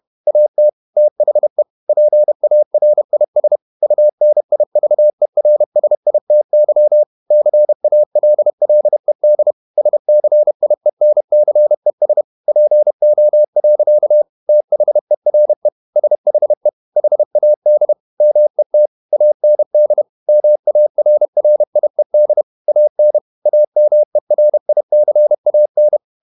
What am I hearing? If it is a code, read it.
Morse code, heard as a keyed tone, about 31 wpm, 600 Hz. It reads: AT THE PARIS UNIVERSITY CALLED SCIENCES PO. THERE SHE HAD MET AND MARRIED AN AMERICAN
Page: Never 31wpm | CW med Gnister